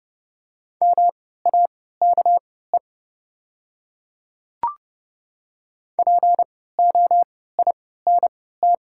Morse code, heard as a keyed tone, about 29 wpm, spaced as for 17 wpm, 690 Hz. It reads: MAKE POINT
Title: 3x Character Spacing - Top 200 Words - 30wpm